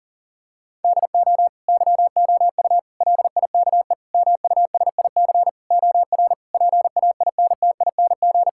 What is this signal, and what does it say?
Morse code, heard as a keyed tone, about 40 wpm, 700 Hz. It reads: DO YOU LIKE MUSIC OR PAINTING
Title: Sentences from Top 1000 Words - Review Part 6 - 40wpm